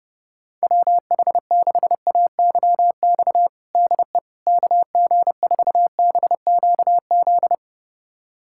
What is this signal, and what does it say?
Morse code, heard as a keyed tone, about 32 wpm, 700 Hz. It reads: WH6AYX DE KG4BQZ
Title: USA Callsign DE Callsign 30wpm